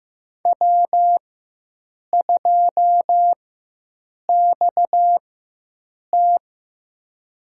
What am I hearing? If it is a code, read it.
Morse code, heard as a keyed tone, about 15 wpm, spaced as for 7 wpm, 700 Hz. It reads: W2XT
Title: USA Call Signs - 4 Characters - 4x Character Spacing - 15wpm